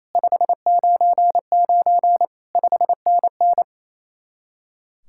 Morse code, heard as a keyed tone, about 34 wpm, 700 Hz. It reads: 599 5NN
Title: Speed-Racing Common QSO Phrases Encore - 28wpm